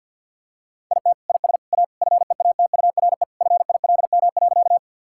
Morse code, heard as a keyed tone, about 50 wpm, 700 Hz. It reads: IT IS A FEATURE FILM.
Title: Sentences from Top 1000 Words - Review Part 5 - 50wpm